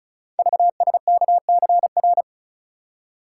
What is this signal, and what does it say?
Morse code, heard as a keyed tone, about 36 wpm, 690 Hz. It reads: VSKCR
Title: Code Groups - 5 Letters - 35wpm